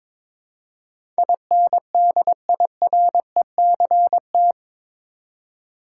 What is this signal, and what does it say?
Morse code, heard as a keyed tone, about 22 wpm, 700 Hz. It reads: INDIRECT
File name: practice / 22wpm / Eight Letter Words - 22wpm